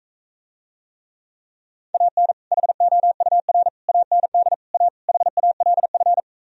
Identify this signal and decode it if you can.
Morse code, heard as a keyed tone, about 45 wpm, 700 Hz. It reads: AN HOUR AND A HALF